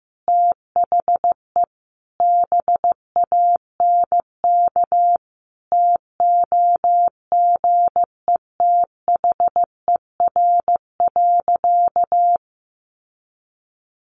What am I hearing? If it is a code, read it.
Morse code, heard as a keyed tone, about 15 wpm, 700 Hz. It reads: THE BANK TOGETHER.
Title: Sentences from Top 900 Words 15wpm